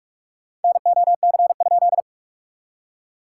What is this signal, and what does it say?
Morse code, heard as a keyed tone, about 41 wpm, 700 Hz. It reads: NOC?
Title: SOTA QSO Practice 45wpm